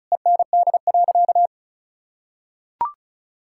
Morse code, heard as a keyed tone, about 34 wpm, 700 Hz. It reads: END.
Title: Sentences from Top 1000 Words - Review Part 5 - 35wpm